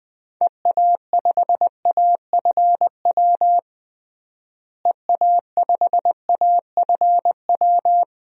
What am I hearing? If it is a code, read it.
Morse code, heard as a keyed tone, about 20 wpm, 700 Hz. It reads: EA5AFW EA5AFW
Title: DXCC-100 Call Signs - 6 Characters - Repeated 2x - 20wpm